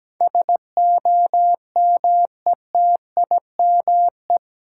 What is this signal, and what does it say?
Morse code, heard as a keyed tone, about 17 wpm, 700 Hz. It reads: SOMETIME
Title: Sentences from Top 1000 Words - Review Part 6 - New Word Drill - 17wpm